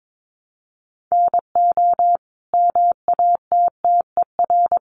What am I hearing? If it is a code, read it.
Morse code, heard as a keyed tone, about 22 wpm, 700 Hz. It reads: NO MATTER
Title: Sentences from Top 1200 Words 22wpm